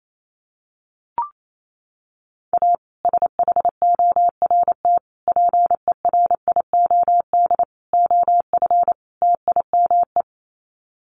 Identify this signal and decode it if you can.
Morse code, heard as a keyed tone, about 28 wpm, 700 Hz. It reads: A SHORT PERIOD OF TIME
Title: Sets of 5 Words Encore 28wpm